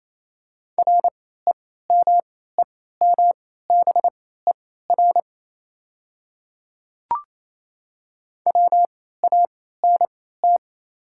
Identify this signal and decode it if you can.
Morse code, heard as a keyed tone, about 28 wpm, spaced as for 16 wpm, 690 Hz. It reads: REMEMBER WANT
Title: 3x Character Spacing - Top 500 Words - 28wpm